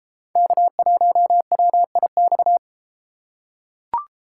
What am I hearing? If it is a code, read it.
Morse code, heard as a keyed tone, about 33 wpm, 700 Hz. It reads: K1WIX